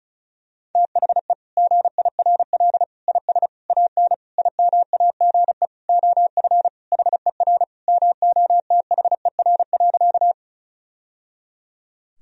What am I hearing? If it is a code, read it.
Morse code, heard as a keyed tone, about 34 wpm, 700 Hz. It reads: THE GIRL IS AN IMAGE OF HER MOTHER.
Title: Sentences from Top 1000 Words - Review Part 4 - 35wpm